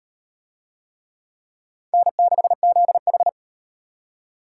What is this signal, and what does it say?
Morse code, heard as a keyed tone, about 38 wpm, 700 Hz. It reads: N6ZH